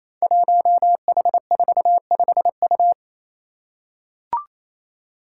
Morse code, heard as a keyed tone, about 27 wpm, 700 Hz. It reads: 1H45U
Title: Five Challenging Letter-Number Combinations 28wpm